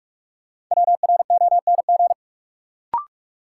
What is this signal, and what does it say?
Morse code, heard as a keyed tone, about 46 wpm, 700 Hz. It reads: WRONG
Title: Sets of 5 Words Encore - New Word Drill - 45wpm